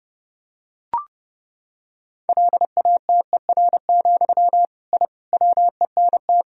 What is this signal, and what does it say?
Morse code, heard as a keyed tone, about 30 wpm, 700 Hz. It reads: LATER, I WENT
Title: Sentences from Top 1000 Words - Review Part 4 - 30wpm